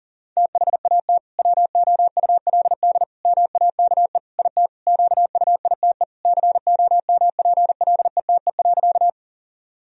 Morse code, heard as a keyed tone, about 37 wpm, 700 Hz. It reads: THAT WOULD MAKE IT QUITE COMPLETE.
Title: Sentences from Top 1000 Words - Review Part 2 - 40wpm